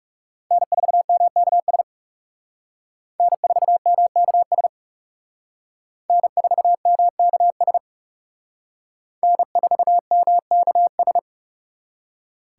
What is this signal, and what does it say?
Morse code, heard as a keyed tone, about 36 wpm, 700 Hz. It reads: N4MKS N4MKS N4MKS N4MKS
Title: Speed Racing - USA 1x3 Call Signs 30wpm